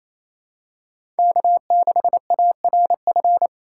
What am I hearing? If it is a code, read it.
Morse code, heard as a keyed tone, about 28 wpm, 700 Hz. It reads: K6ARF
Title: USA Call Signs - 5 Characters - 4x Character Spacing - 28wpm